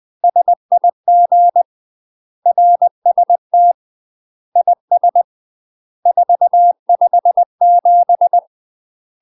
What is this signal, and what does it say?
Morse code, heard as a keyed tone, about 20 wpm, 700 Hz. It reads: SIG RST IS 457
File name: UK MASTER QSOs TRAINING / 20WPM / UK MASTER QSOs 0053